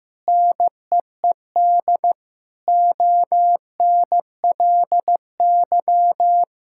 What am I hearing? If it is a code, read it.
Morse code, heard as a keyed tone, about 14 wpm, 700 Hz. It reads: NEED ONLY
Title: Sentences from Top 1000 Words - Review Part 6 - 15wpm